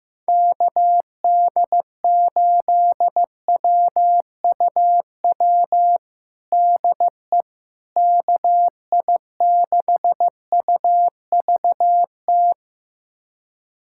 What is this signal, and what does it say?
Morse code, heard as a keyed tone, about 15 wpm, 700 Hz. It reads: KD8WUW DE KI6UVT
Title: USA Callsign DE Callsign - 6 Characters - 15wpm